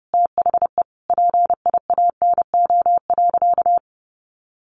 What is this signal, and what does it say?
Morse code, heard as a keyed tone, about 30 wpm, 700 Hz. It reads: THE PIANO.
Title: Additional Easy Phrases 30wpm